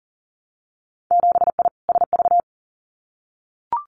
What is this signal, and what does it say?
Morse code, heard as a keyed tone, about 37 wpm, 700 Hz. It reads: 7I SV
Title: Winter Field Day Exchanges Encore 40wpm